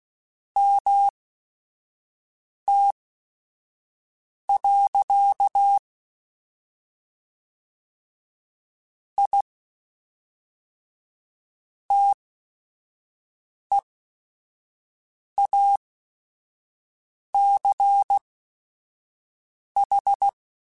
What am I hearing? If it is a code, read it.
Morse code, heard as a keyed tone, about 16 wpm, spaced as for 5 wpm, 800 Hz. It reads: MT. I TEACH
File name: QSOs / QSOs-mp3s / WD0BGP 1@5